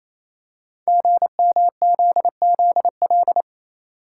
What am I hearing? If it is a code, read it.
Morse code, heard as a keyed tone, about 28 wpm, 700 Hz. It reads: GMZZL
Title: Code Groups - 5 Letters - 28wpm